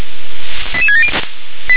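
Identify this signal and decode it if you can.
Five-tone selective call (ZVEI), heard as quick runs of five tones, tones of about 70 ms. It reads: 88670
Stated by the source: ZVEI five-tone selective call received with an SDR.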